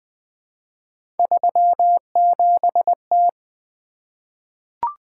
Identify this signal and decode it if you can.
Morse code, heard as a keyed tone, about 20 wpm, 700 Hz. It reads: 37T
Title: Three Letter-Number Combinations - 20wpm